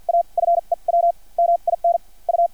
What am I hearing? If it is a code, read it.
Morse code, heard as a keyed tone, about 50 wpm, spaced as for 34 wpm, 690 Hz. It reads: A FEW MINU
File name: archived / CRSnet